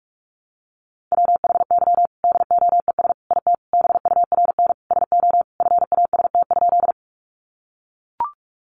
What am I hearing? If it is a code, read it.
Morse code, heard as a keyed tone, about 44 wpm, 700 Hz. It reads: WHY DOES IT BURN SO FAST?
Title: Sentences from Top 1300 Words 45wpm